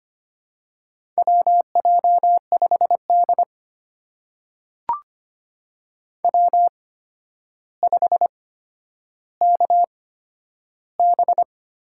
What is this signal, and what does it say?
Morse code, heard as a keyed tone, about 25 wpm, spaced as for 11 wpm, 700 Hz. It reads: WJ5D W5KB
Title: USA Call Signs - 8x Character Spacing - 25wpm